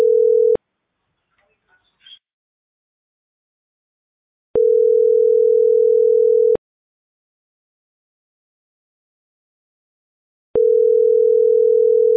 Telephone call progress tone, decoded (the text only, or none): ringback tone